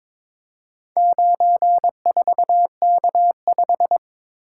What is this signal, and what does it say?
Morse code, heard as a keyed tone, about 22 wpm, 690 Hz. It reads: 94K5
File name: practice / 22wpm / Four Letter-Number Combinations - 22wpm